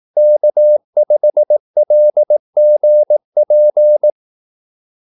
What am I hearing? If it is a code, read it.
Morse code, heard as a keyed tone, about 18 wpm, 600 Hz. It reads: K5LGP
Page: Kaldesignaler 18 wpm